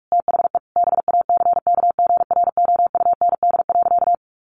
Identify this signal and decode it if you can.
Morse code, heard as a keyed tone, about 44 wpm, 700 Hz. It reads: THE BACKGROUND.
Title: Sentences from Top 1300 Words 45wpm